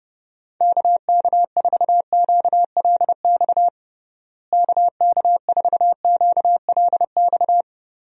Morse code, heard as a keyed tone, about 32 wpm, 700 Hz. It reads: KK4QLX KK4QLX
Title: USA Call Signs - 6 Characters - Repeated 3x - 30wpm